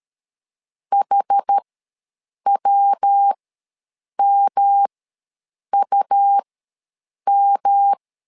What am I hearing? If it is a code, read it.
Morse code, heard as a keyed tone, about 13 wpm, spaced as for 7 wpm, 800 Hz. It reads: HWMUM
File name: LEZIONI / 3 MWRO / 3 MW ws 12wpm 1816